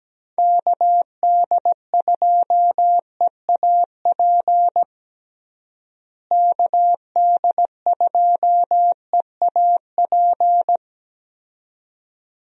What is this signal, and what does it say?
Morse code, heard as a keyed tone, about 17 wpm, 700 Hz. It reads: KD2EAP KD2EAP
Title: USA Call Signs - 6 Characters - Repeated 2x - 17wpm